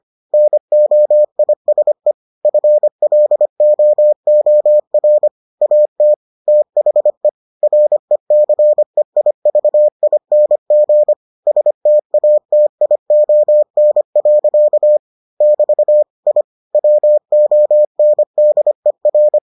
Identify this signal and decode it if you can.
Morse code, heard as a keyed tone, about 25 wpm, 600 Hz. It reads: NOISE FLOOR AT THE RECEIVING STATION. <BT> I WONDER